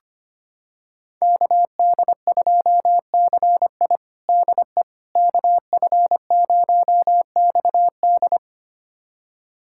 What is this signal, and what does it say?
Morse code, heard as a keyed tone, about 25 wpm, 700 Hz. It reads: KD2CI DE KF0XD
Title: USA Callsign DE Callsign - 5 Characters - 25wpm